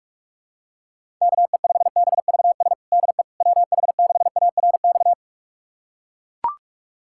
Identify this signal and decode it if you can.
Morse code, heard as a keyed tone, about 41 wpm, 700 Hz. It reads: KE5BVS DE WH6ARX
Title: USA Callsign DE Callsign - 6 Characters - 45wpm